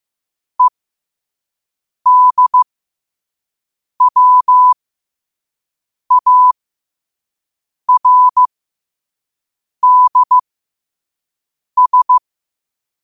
Morse code, heard as a keyed tone, about 15 wpm, spaced as for 5 wpm, 1000 Hz. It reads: EDWARDS
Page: qotd-at-5-wpm